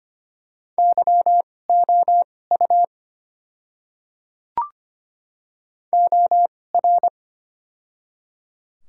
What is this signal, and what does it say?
Morse code, heard as a keyed tone, about 25 wpm, spaced as for 18 wpm, 700 Hz. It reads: YOU OR
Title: 2x Character Spacing - Top 100 Words - 25wpm